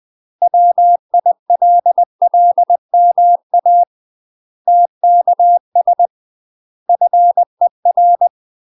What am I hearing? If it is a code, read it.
Morse code, heard as a keyed tone, about 20 wpm, 700 Hz. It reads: WILLMA TKS FER
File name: UK MASTER QSOs TRAINING / 20WPM / UK MASTER QSOs 0057